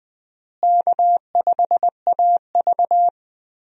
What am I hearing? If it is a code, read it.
Morse code, heard as a keyed tone, about 21 wpm, 700 Hz. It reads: K5AV
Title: USA Call Signs - 4 Characters - Repeated 3x - 20wpm